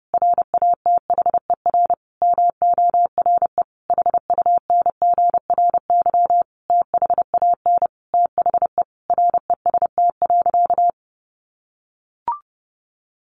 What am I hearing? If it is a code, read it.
Morse code, heard as a keyed tone, about 28 wpm, 700 Hz. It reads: RATHER MORE HUNGRY THAN THE REST.
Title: Sentences from Top 1400 Words 30wpm